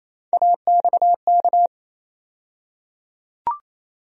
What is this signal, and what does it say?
Morse code, heard as a keyed tone, about 28 wpm, 700 Hz. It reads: AXK
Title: USA Call Sign Postfixes - 3 Letters 28wpm